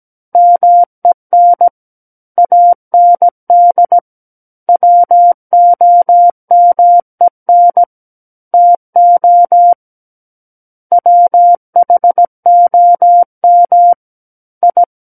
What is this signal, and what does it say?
Morse code, heard as a keyed tone, about 17 wpm, 700 Hz. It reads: MEN AND WOMEN TO WHOM I
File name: Book One / 08-Chapter 8 - 17 WPM